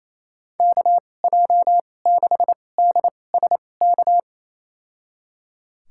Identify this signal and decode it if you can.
Morse code, heard as a keyed tone, about 28 wpm, spaced as for 20 wpm, 700 Hz. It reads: KJ6DSK
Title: USA Call Signs - 2x Character Spacing - 28wpm